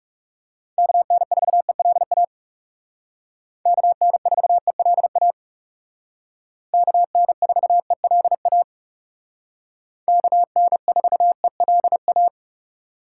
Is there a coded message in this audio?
Yes: Morse code from 0.8 to 12.3 s, heard as a keyed tone, about 39 wpm, 700 Hz. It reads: KN4ELA KN4ELA KN4ELA KN4ELA